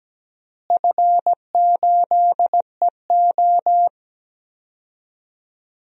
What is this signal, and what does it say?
Morse code, heard as a keyed tone, about 17 wpm, 700 Hz. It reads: F8EO
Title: DXCC-100 Call Signs - 4 Characters - Repeated 3x - 17wpm